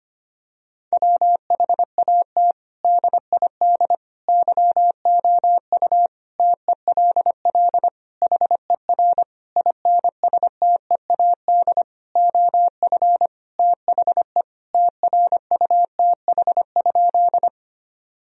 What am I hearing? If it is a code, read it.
Morse code, heard as a keyed tone, about 25 wpm, 700 Hz. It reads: WHAT DID YOU TELL HER INSTEAD OF THE TRUTH?
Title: Sentences from Top 1000 Words - Review Part 4 - 25wpm